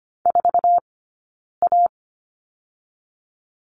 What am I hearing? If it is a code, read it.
Morse code, heard as a keyed tone, about 25 wpm, spaced as for 9 wpm, 700 Hz. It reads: 4A
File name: Lesson 17d - Letter u - Two Characters 25wpm